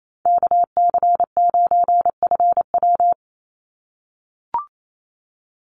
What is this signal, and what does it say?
Morse code, heard as a keyed tone, about 28 wpm, 700 Hz. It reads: KC9FW
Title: USA Call Signs 28wpm